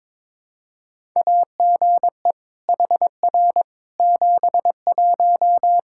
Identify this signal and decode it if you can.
Morse code, heard as a keyed tone, about 22 wpm, 700 Hz. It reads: AGE HR 71
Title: Practice QSO - Exchange 3 - More Info - 22wpm